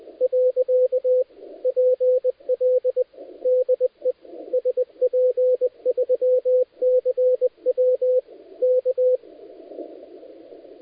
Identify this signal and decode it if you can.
Morse code, heard as a keyed tone, about 20 wpm, 510 Hz, with radio noise. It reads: .PL DE SP3CW K
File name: dlazaawansowanych / nagraniazpasma / titawaNETcontrol